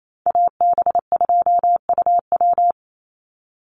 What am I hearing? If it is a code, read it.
Morse code, heard as a keyed tone, about 28 wpm, 700 Hz. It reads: AB2UW
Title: USA Call Signs 28wpm